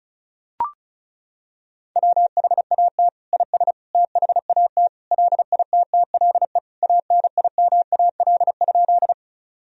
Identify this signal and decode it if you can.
Morse code, heard as a keyed tone, about 36 wpm, 700 Hz. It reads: WHAT IS THAT LITTLE ANIMAL?
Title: Sentences from Top 1000 Words - Review Part 1 - 35wpm